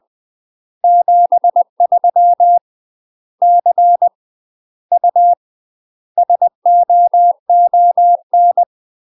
Morse code, heard as a keyed tone, about 20 wpm, 700 Hz. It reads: 73 C U SOON
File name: UK MASTER QSOs TRAINING / 20WPM / UK MASTER QSOs 0062